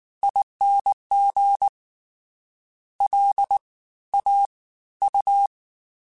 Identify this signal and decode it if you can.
Morse code, heard as a keyed tone, about 19 wpm, spaced as for 14 wpm, 800 Hz. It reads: ING LAU